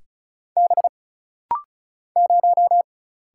Morse code, heard as a keyed tone, about 36 wpm, 700 Hz. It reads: B 0
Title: Single Letter-Number - Rapid-Fire 35wpm